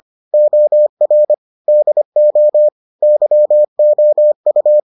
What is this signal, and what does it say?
Morse code, heard as a keyed tone, about 25 wpm, 600 Hz. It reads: OR DO YOU